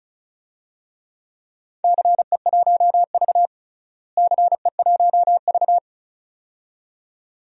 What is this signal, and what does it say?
Morse code, heard as a keyed tone, about 34 wpm, 700 Hz. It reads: CE1V CE1V
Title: DXCC-100 Call Signs - 4 Characters - Repeated 2x - 35wpm